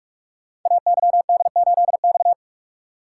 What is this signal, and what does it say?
Morse code, heard as a keyed tone, about 44 wpm, 700 Hz. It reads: AYD8X
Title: Code Groups - 5 Letter:Numbers - 45wpm